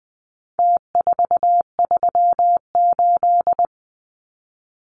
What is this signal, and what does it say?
Morse code, heard as a keyed tone, about 20 wpm, 700 Hz. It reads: T438
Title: Numbers - Contest Style - 4-Digit 20wpm